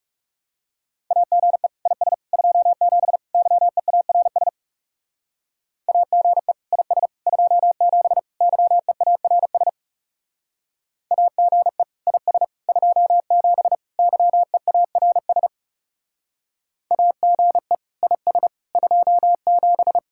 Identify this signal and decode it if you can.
Morse code, heard as a keyed tone, about 35 wpm, 700 Hz. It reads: AGE IS 27 YEARS AGE IS 27 YEARS AGE IS 27 YEARS AGE IS 27
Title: Speed-Racing Common QSO Phrases - 30wpm